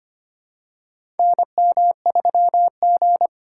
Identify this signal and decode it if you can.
Morse code, heard as a keyed tone, about 25 wpm, 700 Hz. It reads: NM3G
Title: USA Call Signs - 4 Characters - 3x Character Spacing - 25wpm